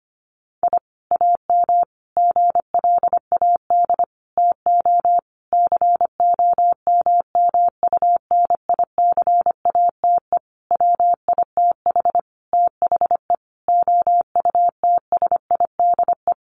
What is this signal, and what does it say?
Morse code, heard as a keyed tone, about 25 wpm, 700 Hz. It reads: I AM GLAD TO COMMUNICATE WITH THE OUTSIDE
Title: Sentences from Top 1300 Words 25wpm